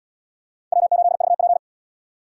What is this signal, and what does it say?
Morse code, heard as a keyed tone, about 73 wpm, 700 Hz. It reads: W8VP